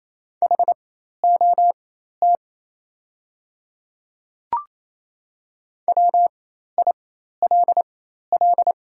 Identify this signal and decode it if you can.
Morse code, heard as a keyed tone, about 28 wpm, spaced as for 13 wpm, 700 Hz. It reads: HOT WILL
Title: 4x Character Spacing - Top 100 Words - 28wpm